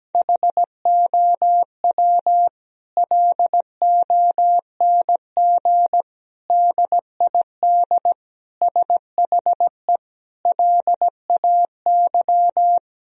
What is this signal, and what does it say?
Morse code, heard as a keyed tone, about 17 wpm, 700 Hz. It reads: HOW LONG DID SHE LAY
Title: Sentences from Top 1000 Words - Review Part 2 - 17wpm